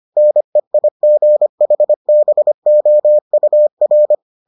Morse code, heard as a keyed tone, about 25 wpm, 600 Hz. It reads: NEIGHBOUR